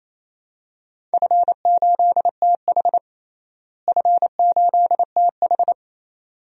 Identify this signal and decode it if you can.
Morse code, heard as a keyed tone, about 28 wpm, 700 Hz. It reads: F8TH F8TH
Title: DXCC-100 Call Signs - 4 Characters - Repeated 2x - 28wpm